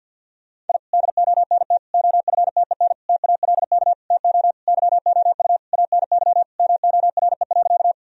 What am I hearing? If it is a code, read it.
Morse code, heard as a keyed tone, about 52 wpm, 700 Hz. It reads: I DONT OFTEN TALK TO YOU ANY MORE.
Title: Sentences from Top 1000 Words - Review Part 1 - 50wpm